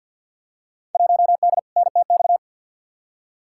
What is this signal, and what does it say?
Morse code, heard as a keyed tone, about 50 wpm, 700 Hz. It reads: 1D NTX
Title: Field Day Exchanges - 50wpm